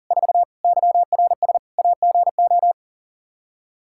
Morse code, heard as a keyed tone, about 40 wpm, 700 Hz. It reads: 4 YRS AGO
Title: Practice QSO - Exchange 3 - More Info - 40wpm